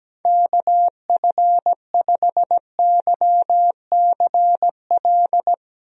Morse code, heard as a keyed tone, about 17 wpm, 700 Hz. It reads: KF5YCL
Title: USA Call Signs - 6 Characters - Repeated 3x - 17wpm